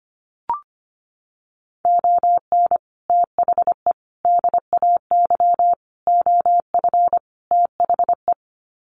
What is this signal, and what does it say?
Morse code, heard as a keyed tone, about 25 wpm, 700 Hz. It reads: ON THE DAY OF THE
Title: Sets of 5 Words Encore 25wpm